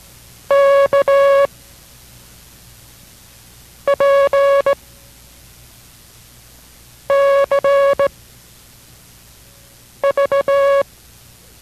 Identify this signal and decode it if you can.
Morse code, heard as a keyed tone, about 18 wpm, 560 Hz. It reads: K P C V